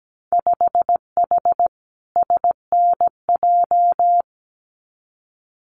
Morse code, heard as a keyed tone, about 17 wpm, 700 Hz. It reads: 5H SNJ
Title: Winter Field Day Exchanges Encore 17wpm